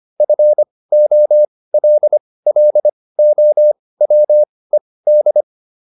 Morse code, heard as a keyed tone, about 25 wpm, spaced as for 18 wpm, 600 Hz. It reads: FOLLOWED